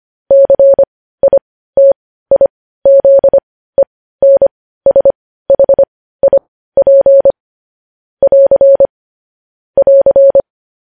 Morse code, heard as a keyed tone, about 25 wpm, spaced as for 15 wpm, 550 Hz. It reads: CITIZENSHIP <AR> <AR>